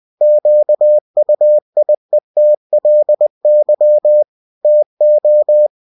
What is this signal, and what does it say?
Morse code, heard as a keyed tone, about 20 wpm, 600 Hz. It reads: QUIETLY TO
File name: cw-books / winnie-the-pooh / winnie-20wpm-page-0041